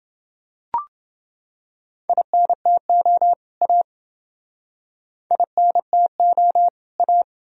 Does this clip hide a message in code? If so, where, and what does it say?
Yes: Morse code from 2.1 to 7.2 s, heard as a keyed tone, about 28 wpm, 700 Hz. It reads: INTO A INTO A